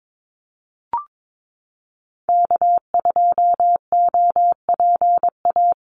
Morse code, heard as a keyed tone, about 22 wpm, 700 Hz. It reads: K2OPA
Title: USA Call Signs Repeated 3x 22wpm